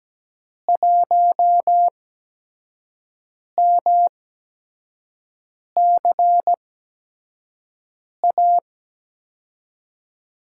Morse code, heard as a keyed tone, about 17 wpm, spaced as for 5 wpm, 700 Hz. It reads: 1MCA
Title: USA Call Signs - 5 Characters - 8x Character Spacing - 17wpm